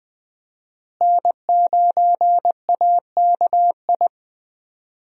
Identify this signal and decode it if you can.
Morse code, heard as a keyed tone, about 21 wpm, 700 Hz. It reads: N9AKI
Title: USA Call Signs - 5 Characters - Repeated 3x - 20wpm